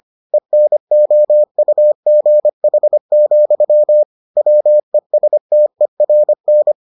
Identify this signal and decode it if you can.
Morse code, heard as a keyed tone, about 25 wpm, 600 Hz. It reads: ENOUGH, WESTERN